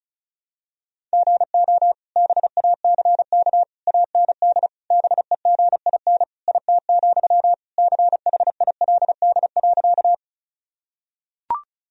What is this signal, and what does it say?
Morse code, heard as a keyed tone, about 36 wpm, 700 Hz. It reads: GO BACK AND BEGIN IT, CHILD.
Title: Sentences from Top 1000 Words - Review Part 1 - 35wpm